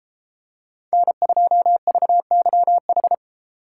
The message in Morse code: N2VYH

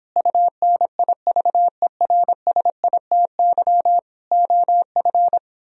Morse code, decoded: UNIVERSITY OF

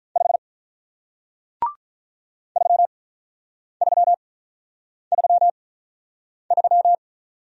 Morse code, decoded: 5 3333

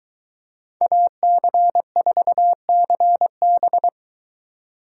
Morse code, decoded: AC4CB

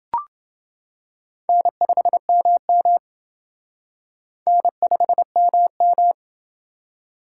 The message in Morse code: N5MM N5MM

